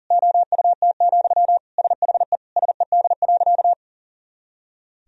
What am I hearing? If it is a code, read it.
Morse code, heard as a keyed tone, about 40 wpm, 700 Hz. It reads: OUT, SHE SED.
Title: Sentences from Top 500 Words and Abbreviations 40wpm